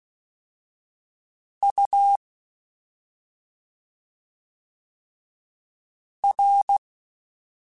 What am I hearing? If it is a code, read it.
Morse code, heard as a keyed tone, about 16 wpm, spaced as for 2 wpm, 800 Hz. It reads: UR
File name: Special INGs-mp3s / Special INGs-mp3s / Slow ing @10